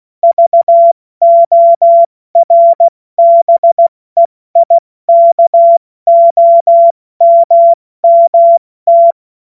Morse code, heard as a keyed tone, about 16 wpm, spaced as for 14 wpm, 680 Hz. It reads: VORBEIKOMMT
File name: Morsekurs / Witze / Witz 20. September 2022 70 F80